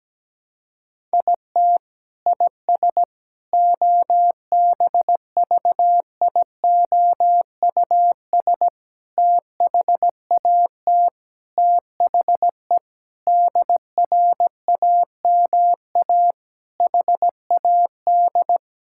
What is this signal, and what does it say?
Morse code, heard as a keyed tone, about 17 wpm, 700 Hz. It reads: IT IS OBVIOUS THAT THE DRAMA HAD
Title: Sentences from Top 1600 Words 17wpm